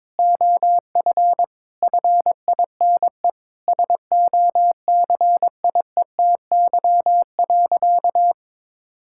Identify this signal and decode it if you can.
Morse code, heard as a keyed tone, about 22 wpm, 700 Hz. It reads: OF FINE SOCIETY.
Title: Sentences from Top 1600 Words 22wpm